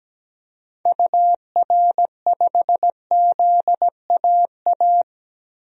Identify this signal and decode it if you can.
Morse code, heard as a keyed tone, about 17 wpm, 700 Hz. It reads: UR5ZAA
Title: DXCC-100 Call Signs - 6 Characters - 17wpm